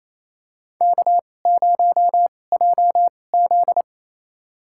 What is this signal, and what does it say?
Morse code, heard as a keyed tone, about 28 wpm, spaced as for 20 wpm, 700 Hz. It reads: K0JZ
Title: USA Call Signs - 4 Characters - 2x Character Spacing - 28wpm